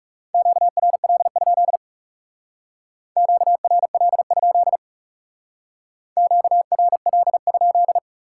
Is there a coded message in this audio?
Yes: Morse code from 0.3 to 8.0 s, heard as a keyed tone, about 40 wpm, 700 Hz. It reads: QRL? QRL? QRL?